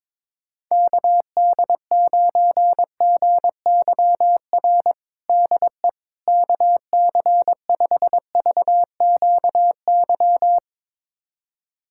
Morse code, heard as a keyed tone, about 23 wpm, 700 Hz. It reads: KD9GYR DE KC5VQY
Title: USA Callsign DE Callsign - 6 Characters - 22wpm